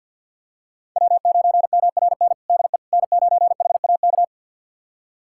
Morse code, heard as a keyed tone, about 48 wpm, 690 Hz. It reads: W9MRN DE N9HAK
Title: USA Callsign DE Callsign - 5 Characters - 50wpm